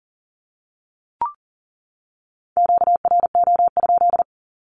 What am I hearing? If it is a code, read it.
Morse code, heard as a keyed tone, about 40 wpm, 700 Hz. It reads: QRO?
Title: Q-Codes 40wpm